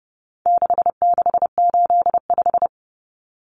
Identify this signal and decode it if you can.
Morse code, heard as a keyed tone, about 30 wpm, 700 Hz. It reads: 6685
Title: Common Cut Numbers - 4-Digit 30wpm